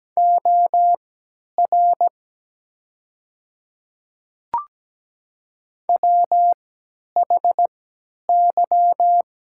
Morse code, heard as a keyed tone, about 17 wpm, spaced as for 10 wpm, 690 Hz. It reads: OR WHY
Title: 3x Character Spacing - Top 300 Words - 17wpm